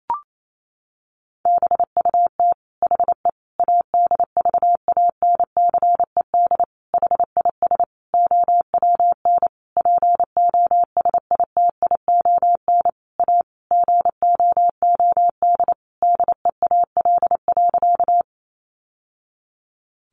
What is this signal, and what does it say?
Morse code, heard as a keyed tone, about 28 wpm, 700 Hz. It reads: BUT HE ADVANCED HIS OWN POSITION A GOOD DEAL.
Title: Sentences from Top 1200 Words 28wpm